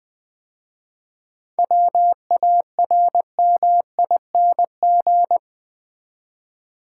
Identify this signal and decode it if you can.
Morse code, heard as a keyed tone, about 20 wpm, 700 Hz. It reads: WARMING
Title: 7 Letter Words - First 3 Letters Given - 20wpm